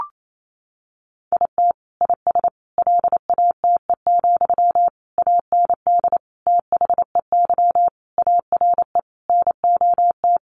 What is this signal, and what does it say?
Morse code, heard as a keyed tone, about 28 wpm, 700 Hz. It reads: IT IS LATE, AND THEY ARE NOT
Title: Sentences from Top 600 Words 28wpm